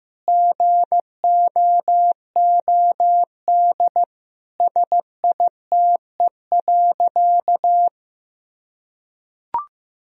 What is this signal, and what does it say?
Morse code, heard as a keyed tone, about 16 wpm, 700 Hz. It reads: GOOD SITE.
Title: Sentences from Top 1000 Words - Review Part 4 - 15wpm